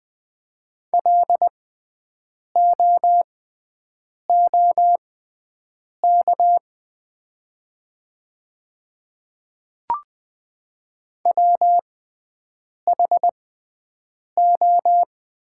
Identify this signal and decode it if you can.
Morse code, heard as a keyed tone, about 20 wpm, spaced as for 7 wpm, 700 Hz. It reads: LOOK WHO
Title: 6x Character Spacing - Top 100 Words - 20wpm